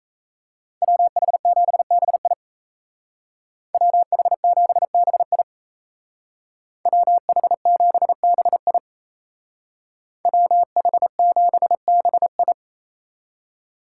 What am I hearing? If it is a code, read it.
Morse code, heard as a keyed tone, about 37 wpm, 700 Hz. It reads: WH7BI WH7BI WH7BI WH7BI